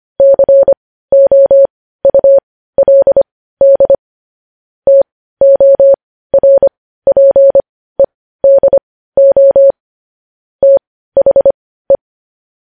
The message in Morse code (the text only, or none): COULD TORPEDO THE